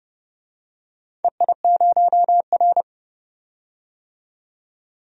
Morse code, heard as a keyed tone, about 31 wpm, 700 Hz. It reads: EI0R
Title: DXCC-100 Call Signs - 4 Characters - 30wpm